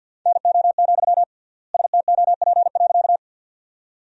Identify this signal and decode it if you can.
Morse code, heard as a keyed tone, about 47 wpm, 700 Hz. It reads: NO, STOP.
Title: 3x Word-Spacing - Sentences from Top 300 Words - 50wpm